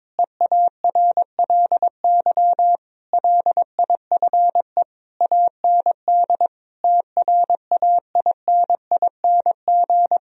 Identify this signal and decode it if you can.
Morse code, heard as a keyed tone, about 22 wpm, 700 Hz. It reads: EARLY LIFE AND TRAINING
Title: Sentences from Top 1000 Words - Review Part 4 - 22wpm